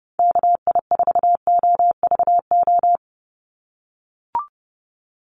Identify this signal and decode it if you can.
Morse code, heard as a keyed tone, about 30 wpm, 700 Hz. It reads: KI4OVO
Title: USA Call Signs 30wpm